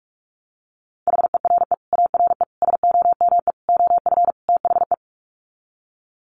Morse code, heard as a keyed tone, about 44 wpm, 700 Hz. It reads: HERE ARE SOME OF THE
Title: Sets of 5 Words Encore 45wpm